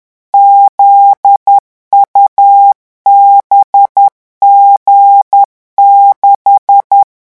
Morse code, heard as a keyed tone, about 10 wpm, 800 Hz. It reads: ZUBG6
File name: morse mp3 / revision13-vitesse 10 mots